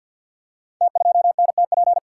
Morse code, heard as a keyed tone, about 52 wpm, 690 Hz. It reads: T2NTP
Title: Code Groups - 5 Letter:Numbers - 50wpm